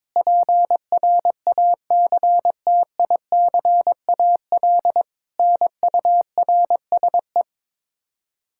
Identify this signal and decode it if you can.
Morse code, heard as a keyed tone, about 22 wpm, 700 Hz. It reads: PRACTICAL NURSE
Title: Practice QSO - Exchange 3 - More Info - 22wpm